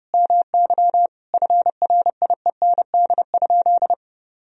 Morse code, heard as a keyed tone, about 32 wpm, 700 Hz. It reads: MY FRIEND?
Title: Sentences from Top 1000 Words - Review Part 3 - 30wpm